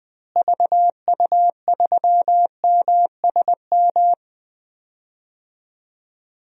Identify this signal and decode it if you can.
Morse code, heard as a keyed tone, about 20 wpm, 700 Hz. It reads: VU3MSM
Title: DXCC-100 Call Signs - 6 Characters - Repeated 3x - 20wpm